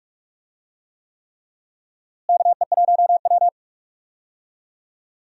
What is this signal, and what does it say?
Morse code, heard as a keyed tone, about 41 wpm, 700 Hz. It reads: KE1W
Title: USA Call Signs - 4 Characters - Repeated 3x - 45wpm